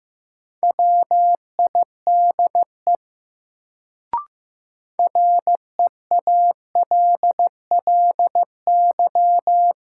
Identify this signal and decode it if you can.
Morse code, heard as a keyed tone, about 15 wpm, 700 Hz. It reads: WIDE REALLY
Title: Sets of 4 Words - New Word Drill - 15wpm